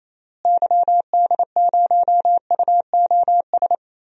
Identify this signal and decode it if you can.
Morse code, heard as a keyed tone, about 28 wpm, 700 Hz. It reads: YD0UOS
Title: DXCC-100 Call Signs - 6 Characters - Repeated 2x - 28wpm